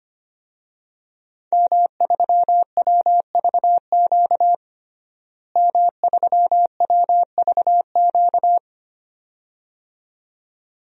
Morse code, heard as a keyed tone, about 25 wpm, 700 Hz. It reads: M3WVQ M3WVQ
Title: DXCC-100 Call Signs - 5 Characters - Repeated 2x - 25wpm